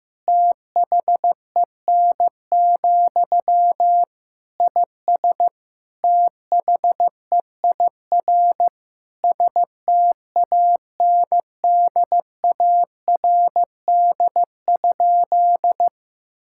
Morse code, heard as a keyed tone, about 15 wpm, 700 Hz. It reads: THEN, IS THEIR STANDARD?
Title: Sentences from Top 1000 Words - Review Part 6 - 15wpm